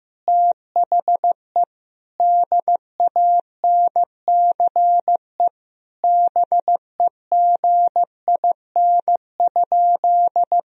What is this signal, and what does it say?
Morse code, heard as a keyed tone, about 15 wpm, 700 Hz. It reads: THE DANCE BEGIN?
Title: Sentences from Top 1000 Words - Review Part 4 - 15wpm